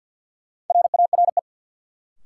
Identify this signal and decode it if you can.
Morse code, heard as a keyed tone, about 50 wpm, 700 Hz. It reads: RARE
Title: Sentences from Top 1100 Words - New Word Drill - 50wpm